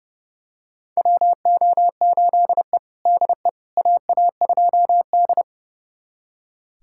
Morse code, heard as a keyed tone, about 30 wpm, 700 Hz. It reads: WO8E DE AA2D
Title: USA Callsign DE Callsign - 4 Characters - 30wpm